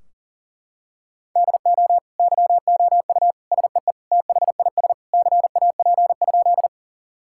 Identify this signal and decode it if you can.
Morse code, heard as a keyed tone, about 40 wpm, 700 Hz. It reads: DO YOU SEE THIS CAP?
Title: Sentences from Top 1600 Words 40wpm